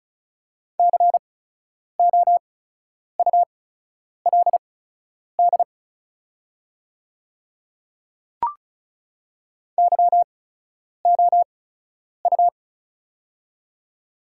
Morse code, heard as a keyed tone, about 34 wpm, spaced as for 10 wpm, 700 Hz. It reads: COULD YOU
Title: 8x Character Spacing - Top 100 Words - 35wpm